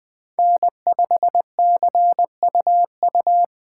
Morse code, heard as a keyed tone, about 20 wpm, 700 Hz. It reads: N5CUU
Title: Speed Racing - USA Call Signs 17wpm